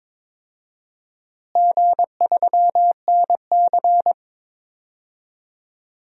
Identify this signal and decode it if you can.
Morse code, heard as a keyed tone, about 22 wpm, 700 Hz. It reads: G3NC
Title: DXCC-100 Call Signs - 4 Characters - Repeated 3x - 22wpm